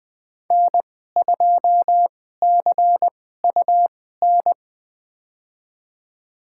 Morse code, heard as a keyed tone, about 20 wpm, spaced as for 15 wpm, 700 Hz. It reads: N2CUN
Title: USA Call Signs - 2x Character Spacing - 20wpm